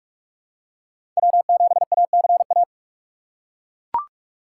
Morse code, heard as a keyed tone, about 42 wpm, 700 Hz. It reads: W7ACA
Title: USA Call Signs - 5 Characters - 3x Character Spacing - 45wpm